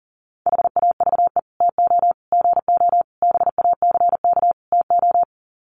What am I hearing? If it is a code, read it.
Morse code, heard as a keyed tone, about 40 wpm, 700 Hz. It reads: HAVE TO GO BACK TO
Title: Sets of 5 Words Encore2 40wpm